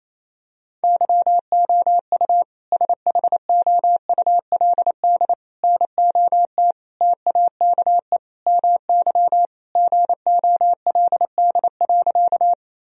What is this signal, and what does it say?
Morse code, heard as a keyed tone, about 28 wpm, 700 Hz. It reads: YOU SHOULD NOT TAKE MY GOLD.
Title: Sentences from Top 1000 Words - Review Part 3 - 28wpm